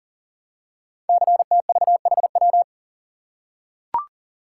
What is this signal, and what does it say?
Morse code, heard as a keyed tone, about 42 wpm, 700 Hz. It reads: CTVHW
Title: Code Groups - 5 Letters - 40wpm